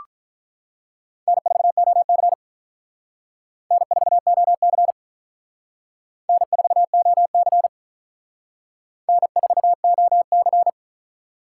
Morse code, heard as a keyed tone, about 44 wpm, 700 Hz. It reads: N4OC N4OC N4OC N4OC